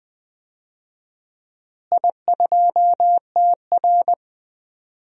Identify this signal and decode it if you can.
Morse code, heard as a keyed tone, about 20 wpm, 700 Hz. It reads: I2TR
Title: DXCC-100 Call Signs - 4 Characters - Repeated 3x - 20wpm